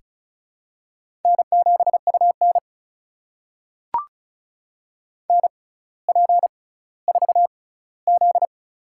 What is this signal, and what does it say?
Morse code, heard as a keyed tone, about 34 wpm, spaced as for 18 wpm, 700 Hz. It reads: N7UN NP4Z
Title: USA Call Signs - 4 Characters - 6x Character Spacing - 35wpm